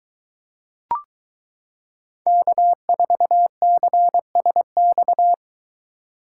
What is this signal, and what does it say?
Morse code, heard as a keyed tone, about 22 wpm, 700 Hz. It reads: K4CSX